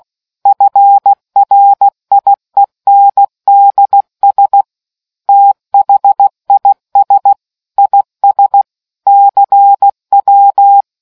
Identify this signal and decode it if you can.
Morse code, heard as a keyed tone, about 16 wpm, 800 Hz. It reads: FRIENDS THIS IS CW